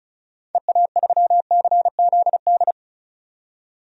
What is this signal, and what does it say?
Morse code, heard as a keyed tone, about 36 wpm, 700 Hz. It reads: EA3CZD
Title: DXCC-100 Call Signs - 6 Characters - 35wpm